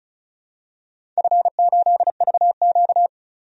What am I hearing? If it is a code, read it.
Morse code, heard as a keyed tone, about 36 wpm, 700 Hz. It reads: F8VQ
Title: DXCC-100 Call Signs - 4 Characters - 35wpm